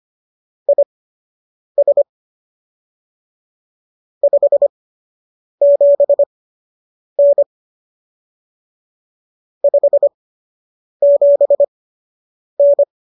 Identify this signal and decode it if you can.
Morse code, heard as a keyed tone, about 25 wpm, spaced as for 8 wpm, 570 Hz. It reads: IS 57N 57N